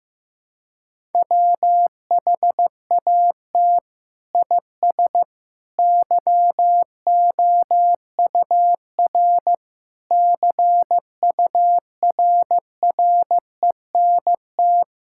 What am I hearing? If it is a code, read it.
Morse code, heard as a keyed tone, about 15 wpm, 700 Hz. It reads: WHAT IS YOUR CURRENT
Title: Sentences from Top 1000 Words - Review Part 6 - 15wpm